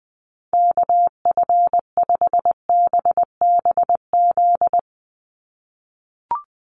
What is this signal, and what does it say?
Morse code, heard as a keyed tone, about 20 wpm, 700 Hz. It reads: KF5BBZ
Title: USA Call Signs 20wpm